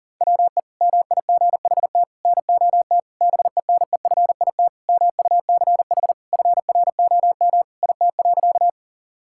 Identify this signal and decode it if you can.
Morse code, heard as a keyed tone, about 40 wpm, 700 Hz. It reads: WE MIGHT NOT BENEFIT MUCH FROM IT.
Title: Sentences from Top 1000 Words - Review Part 6 - 40wpm